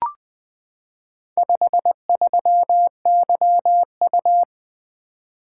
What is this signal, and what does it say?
Morse code, heard as a keyed tone, about 20 wpm, 700 Hz. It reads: 53YU
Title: Four Letter-Number Combinations - 20wpm